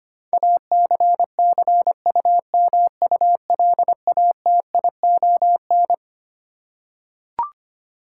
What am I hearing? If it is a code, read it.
Morse code, heard as a keyed tone, about 25 wpm, 700 Hz. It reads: ACCUMULATION
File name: Twelve Letter Words - 25wpm